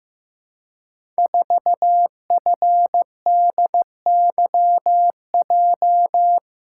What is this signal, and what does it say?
Morse code, heard as a keyed tone, about 15 wpm, 700 Hz. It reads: 4FDYJ
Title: Code Groups - 5 Letter:Numbers - 15wpm